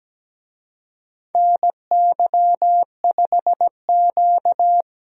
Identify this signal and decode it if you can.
Morse code, heard as a keyed tone, about 17 wpm, 700 Hz. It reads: NY5Q
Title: USA Call Signs - 4 Characters - 17wpm